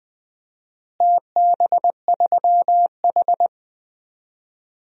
Morse code, heard as a keyed tone, about 20 wpm, 700 Hz. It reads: TB3H
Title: DXCC-100 Call Signs - 4 Characters - Repeated 3x - 20wpm